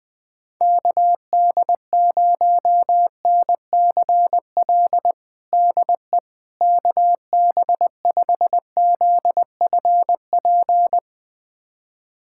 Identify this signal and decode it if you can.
Morse code, heard as a keyed tone, about 20 wpm, 700 Hz. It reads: KD0NCL DE KB5ZFP
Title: USA Callsign DE Callsign - 6 Characters - 20wpm